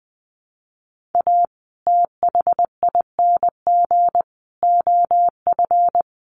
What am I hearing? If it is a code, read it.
Morse code, heard as a keyed tone, about 20 wpm, 700 Hz. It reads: A THING OF
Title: Sets of 5 Words Encore 20wpm